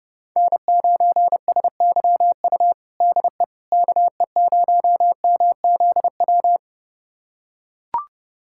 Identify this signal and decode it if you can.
Morse code, heard as a keyed tone, about 30 wpm, 700 Hz. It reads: N9SYU DE KE0MZW
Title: USA Callsign DE Callsign 30wpm